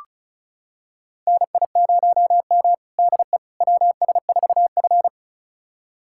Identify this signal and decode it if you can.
Morse code, heard as a keyed tone, about 34 wpm, 700 Hz. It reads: NI0M DE WS4F
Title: USA Callsign DE Callsign - 4 Characters - 35wpm